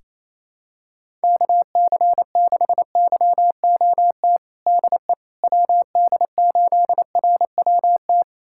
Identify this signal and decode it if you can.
Morse code, heard as a keyed tone, about 28 wpm, 700 Hz. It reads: KC6YOT DE WD8RWT
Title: USA Callsign DE Callsign 28wpm